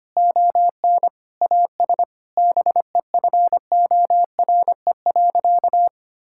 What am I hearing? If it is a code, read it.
Morse code, heard as a keyed tone, about 25 wpm, 700 Hz. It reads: ON AS BEFORE.
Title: Sentences from Top 1000 Words - Review Part 6 - 25wpm